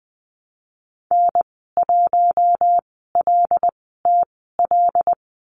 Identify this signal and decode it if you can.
Morse code, heard as a keyed tone, about 20 wpm, spaced as for 15 wpm, 700 Hz. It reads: N1LTL
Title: 076 Lesson 13g - Letter L - Call Signs 20wpm